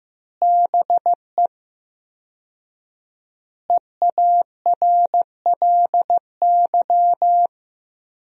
Morse code, heard as a keyed tone, about 15 wpm, 700 Hz. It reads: BE EARLY
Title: 4x Word-Spacing - Sentences from Top 400 Words - 15wpm